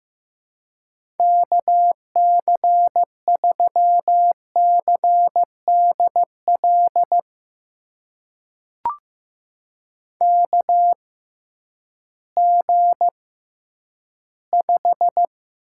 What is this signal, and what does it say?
Morse code, heard as a keyed tone, about 15 wpm, spaced as for 10 wpm, 700 Hz. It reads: KC3CDL KG5
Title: USA Call Signs - 6 Characters - 6x Character Spacing - 15wpm